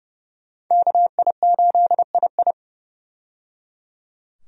Morse code, heard as a keyed tone, about 30 wpm, 700 Hz. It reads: KI8II
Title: USA Call Signs - 5 Characters - 30wpm